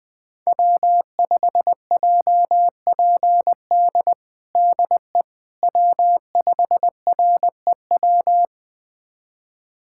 Morse code, heard as a keyed tone, about 20 wpm, 700 Hz. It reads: W5JPD DE W5REW
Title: USA Callsign DE Callsign - 5 Characters - 20wpm